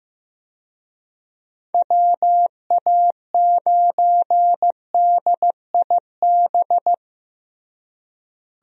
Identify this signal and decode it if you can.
Morse code, heard as a keyed tone, about 16 wpm, 700 Hz. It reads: WA9DIB
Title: USA Call Signs - 6 Characters - Repeated 2x - 15wpm